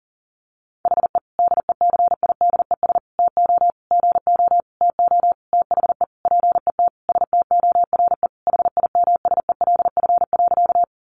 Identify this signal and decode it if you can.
Morse code, heard as a keyed tone, about 40 wpm, 700 Hz. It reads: HE DECIDES TO GO TO THE PET STORE HIMSELF.
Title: Additional Easy Phrases 40wpm